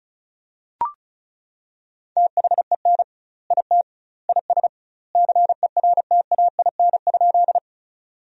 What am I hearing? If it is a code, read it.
Morse code, heard as a keyed tone, about 36 wpm, 700 Hz. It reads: THEN IT IS CERTAIN?
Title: 2x Word-Spacing - Sentences from Top 500 Words - 35wpm